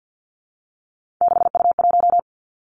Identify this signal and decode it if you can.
Morse code, heard as a keyed tone, about 50 wpm, 700 Hz. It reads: 6U1
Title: Three Challenging Letter-Number Combinations 50wpm